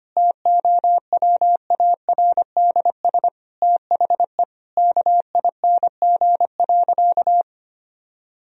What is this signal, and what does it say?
Morse code, heard as a keyed tone, about 25 wpm, 700 Hz. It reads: TOWARDS THE KING.
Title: Sentences from Top 1000 Words - Review Part 2 - 25wpm